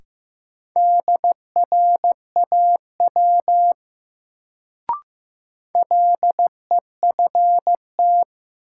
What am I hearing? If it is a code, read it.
Morse code, heard as a keyed tone, about 15 wpm, 700 Hz. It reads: DRAW LEFT
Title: Sentences from Top 300 Words - New Word Drill - 15wpm